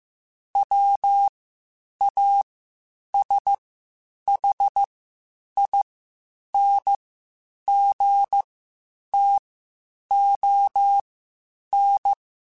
Morse code, heard as a keyed tone, about 15 wpm, spaced as for 8 wpm, 800 Hz. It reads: WASHINGTON
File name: QSOs / QSOs-mp3s / WD0BGP 1@10